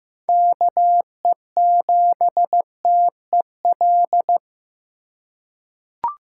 Morse code, heard as a keyed tone, about 15 wpm, 700 Hz. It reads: KE7TEL
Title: USA Call Signs - 6 Characters - 8x Character Spacing - 15wpm